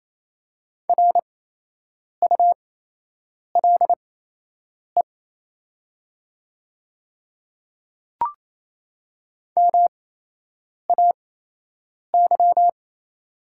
Morse code, heard as a keyed tone, about 28 wpm, spaced as for 8 wpm, 700 Hz. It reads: RULE MAY
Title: 8x Character Spacing - Top 500 Words - 28wpm